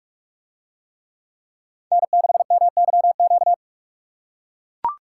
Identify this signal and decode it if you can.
Morse code, heard as a keyed tone, about 44 wpm, 700 Hz. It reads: N6MYQ